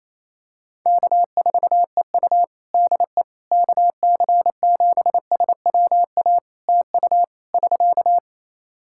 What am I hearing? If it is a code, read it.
Morse code, heard as a keyed tone, about 28 wpm, 700 Hz. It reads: K4EU DE KC7SWA TU <SK>
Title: Practice QSO - Exchange 4 - Ending - 28wpm